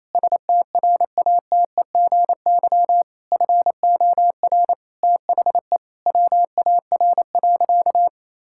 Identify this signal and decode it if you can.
Morse code, heard as a keyed tone, about 28 wpm, 700 Hz. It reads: STRATEGY FOR THE WAR.
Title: Sentences from Top 1000 Words - Review Part 5 - 28wpm